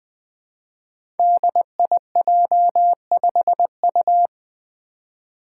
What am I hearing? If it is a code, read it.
Morse code, heard as a keyed tone, about 20 wpm, 700 Hz. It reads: DIJ5U
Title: Five Challenging Letter-Number Combinations 20wpm